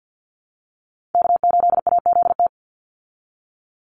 Morse code, heard as a keyed tone, about 50 wpm, 700 Hz. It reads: K8AZT
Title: Call Signs Repeated 3x - 50wpm